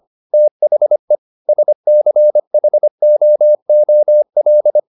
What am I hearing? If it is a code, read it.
Morse code, heard as a keyed tone, about 25 wpm, 600 Hz. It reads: THE SCHOOL